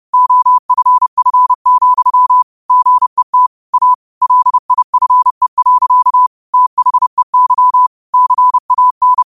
Morse code, heard as a keyed tone, about 30 wpm, 1000 Hz. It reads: OFF, GET A LIFE. THEY CAN
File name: 30wpm